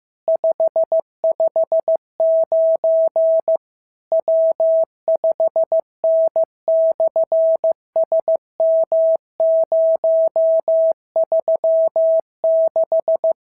Morse code, heard as a keyed tone, about 15 wpm, 650 Hz. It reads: 559 W5N/SM036
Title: SOTA QSO Practice 15wpm